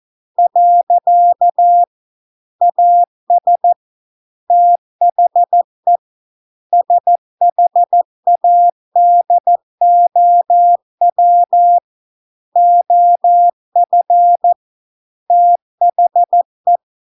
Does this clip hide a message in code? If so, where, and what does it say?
Yes: Morse code from 0.4 to 16.8 s, heard as a keyed tone, about 14 wpm, 700 Hz. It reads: . AS THE SHADOW OF THE